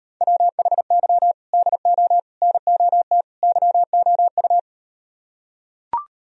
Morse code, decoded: WHY DO NOT YOU

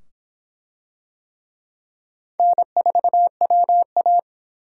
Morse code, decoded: N4WA

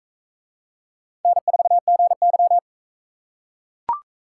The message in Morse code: N4GY

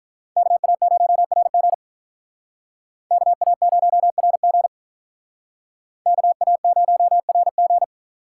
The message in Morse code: KA0RG KA0RG KA0RG

